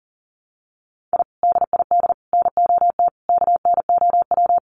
Morse code, heard as a keyed tone, about 40 wpm, 700 Hz. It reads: I DID NOT KNOW
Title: Sets of 4 Words 40wpm